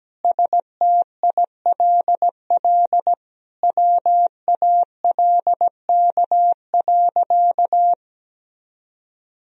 Morse code, heard as a keyed tone, about 17 wpm, 700 Hz. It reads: STILL WALK.
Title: Sentences from Top 1000 Words - Review Part 1 - 17wpm